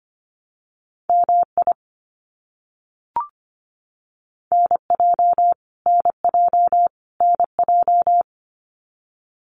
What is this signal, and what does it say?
Morse code, heard as a keyed tone, about 25 wpm, 700 Hz. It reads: MI NJ NJ NJ
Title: US States Repeated 3x 25wpm